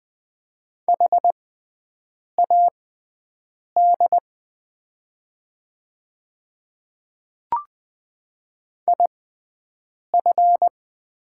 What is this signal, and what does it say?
Morse code, heard as a keyed tone, about 20 wpm, spaced as for 7 wpm, 700 Hz. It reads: HAD IF
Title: 6x Character Spacing - Top 200 Words - 20wpm